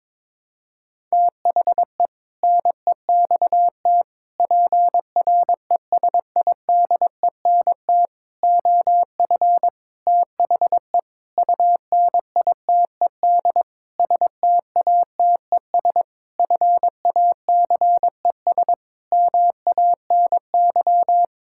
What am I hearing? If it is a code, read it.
Morse code, heard as a keyed tone, about 22 wpm, 700 Hz. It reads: THE NEXT PRESIDENT OF THE UNITED STATES FACES MANY
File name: practice / 22wpm / Sentences with 7-Word Phrases - 22wpm